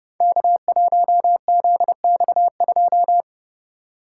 Morse code, heard as a keyed tone, about 29 wpm, 700 Hz. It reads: K1ZX2
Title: Code Groups - 5 Letter:Numbers - 30wpm